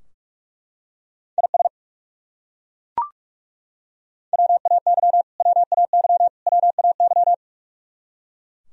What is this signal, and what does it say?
Morse code, heard as a keyed tone, about 41 wpm, 700 Hz. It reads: IS WAY WAY WAY
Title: Top 100 Words Repeated 45wpm